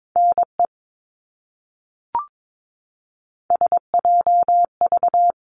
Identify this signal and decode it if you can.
Morse code, heard as a keyed tone, about 22 wpm, 700 Hz. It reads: NE SJV
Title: ARRL Sections 22wpm